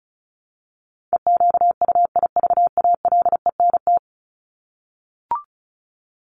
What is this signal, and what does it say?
Morse code, heard as a keyed tone, about 34 wpm, 700 Hz. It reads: EQUIVALENT
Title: Top 1501-1750 Words 35wpm